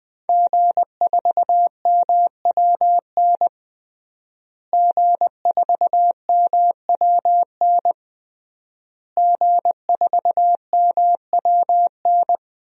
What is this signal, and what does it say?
Morse code, heard as a keyed tone, about 20 wpm, 700 Hz. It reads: G4MWN G4MWN G4MWN
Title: DXCC-100 Call Signs - 5 Characters - Repeated 3x - 20wpm